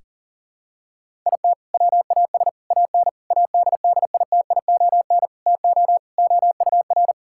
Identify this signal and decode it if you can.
Morse code, heard as a keyed tone, about 40 wpm, 700 Hz. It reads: IT WAS AN ADDITION TO OUR
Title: Sentences from Top 1000 Words - Review Part 6 - 40wpm